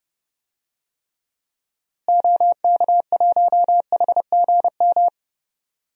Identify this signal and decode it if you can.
Morse code, heard as a keyed tone, about 32 wpm, 700 Hz. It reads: OK1HGM
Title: DXCC-100 Call Signs - 6 Characters - Repeated 2x - 30wpm